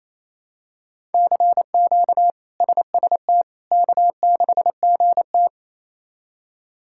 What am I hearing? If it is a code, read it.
Morse code, heard as a keyed tone, about 27 wpm, 700 Hz. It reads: CQ SST K6GT
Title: K1USN SST QSO Practice - 28wpm